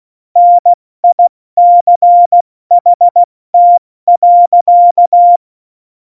Morse code, heard as a keyed tone, about 16 wpm, 700 Hz. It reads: NICHT.
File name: Morsekurs / Witze / Witz 28. November 2017, 70 F80